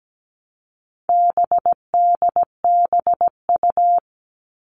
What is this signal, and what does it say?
Morse code, heard as a keyed tone, about 17 wpm, 700 Hz. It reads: BDBU
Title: All Cut Numbers - 4-Digit 17wpm